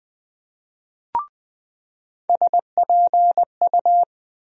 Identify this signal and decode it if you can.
Morse code, heard as a keyed tone, about 20 wpm, 700 Hz. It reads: SPU
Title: USA Call Sign Postfixes - 3 Letters 20wpm